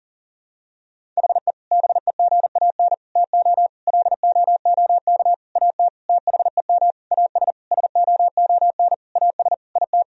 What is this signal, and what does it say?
Morse code, heard as a keyed tone, about 40 wpm, 700 Hz. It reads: HE BEGAN TO LOOK AT THEM AS SOON AS IT
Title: Sentences from Top 1000 Words - Review Part 1 - 40wpm